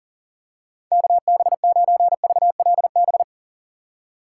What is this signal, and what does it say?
Morse code, heard as a keyed tone, about 43 wpm, 700 Hz. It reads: KB9VLB
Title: USA Call Signs - 6 Characters - 6x Character Spacing - 40wpm